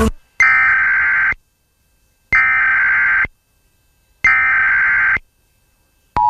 Emergency Alert System (SAME) tones (header bursts), heard as harsh buzzing bursts. It is heard from 0.4 to 5.2 s.